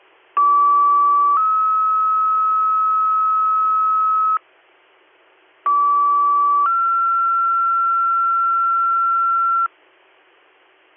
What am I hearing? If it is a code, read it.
Two-tone page, heard as a one-second tone, then a three-second tone, about 1150 then 1290 Hz, about 1150 then 1360 Hz. It is heard from 0.4 to 9.7 s.